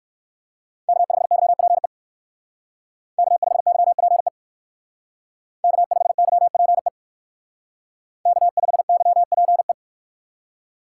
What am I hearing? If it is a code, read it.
Morse code, heard as a keyed tone, about 59 wpm, 700 Hz. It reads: K5YPE K5YPE K5YPE K5YPE